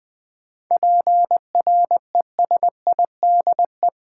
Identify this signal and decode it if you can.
Morse code, heard as a keyed tone, about 20 wpm, 700 Hz. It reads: PRESIDE
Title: 7 Letter Words - First 3 Letters Given - 20wpm